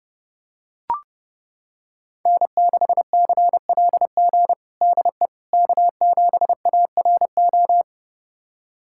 Morse code, heard as a keyed tone, about 31 wpm, 700 Hz. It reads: N6CLG DE K7ARO
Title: USA Callsign DE Callsign - 5 Characters - 30wpm